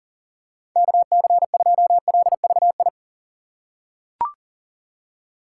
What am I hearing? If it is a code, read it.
Morse code, heard as a keyed tone, about 38 wpm, 700 Hz. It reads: KC2LVI
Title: USA Call Signs - 6 Characters - 4x Character Spacing - 40wpm